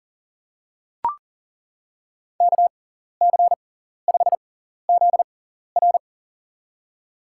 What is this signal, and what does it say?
Morse code, heard as a keyed tone, about 40 wpm, spaced as for 14 wpm, 700 Hz. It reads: KC5ZR
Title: USA Call Signs - 5 Characters - 6x Character Spacing - 40wpm